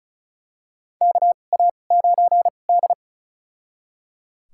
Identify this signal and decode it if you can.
Morse code, heard as a keyed tone, about 36 wpm, spaced as for 25 wpm, 700 Hz. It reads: KA9D
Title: USA Call Signs - 4 Characters - 2x Character Spacing - 35wpm